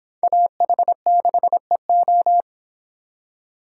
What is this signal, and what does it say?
Morse code, heard as a keyed tone, about 24 wpm, 700 Hz. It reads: AH6EO